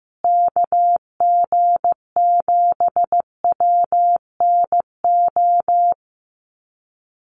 Morse code, heard as a keyed tone, about 15 wpm, 700 Hz. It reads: KG7WNO
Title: USA Call Signs Repeated 2x 15wpm